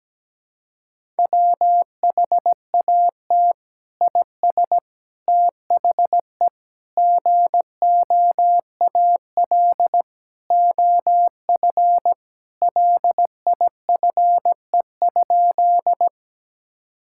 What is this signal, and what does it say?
Morse code, heard as a keyed tone, about 17 wpm, 700 Hz. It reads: WHAT IS THE GOAL OF LIFE?
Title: Sentences from Top 1000 Words - Review Part 5 - 17wpm